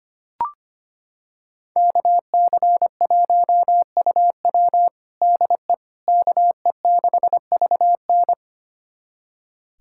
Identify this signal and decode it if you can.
Morse code, heard as a keyed tone, about 25 wpm, 700 Hz. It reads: KC1UW DE KE6VN
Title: USA Callsign DE Callsign - 5 Characters - 25wpm